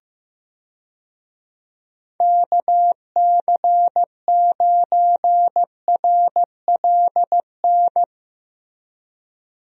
Morse code, heard as a keyed tone, about 15 wpm, 700 Hz. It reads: KC9RLN
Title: USA Call Signs - 6 Characters - Repeated 2x - 15wpm